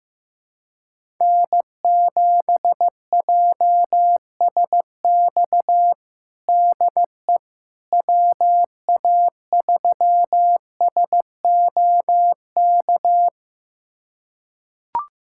Morse code, heard as a keyed tone, about 15 wpm, 700 Hz. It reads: N7JSX DE WA3SOK
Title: USA Callsign DE Callsign 15wpm